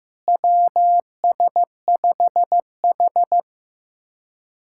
Morse code, heard as a keyed tone, about 15 wpm, 700 Hz. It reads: WS5H
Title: USA Callsign DE Callsign - 4 Characters - 15wpm